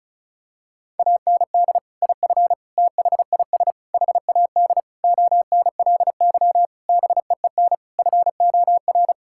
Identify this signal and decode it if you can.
Morse code, heard as a keyed tone, about 36 wpm, 700 Hz. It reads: AND IF THIS HAD ONLY BEEN FOR
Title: Sentences from Top 1600 Words 35wpm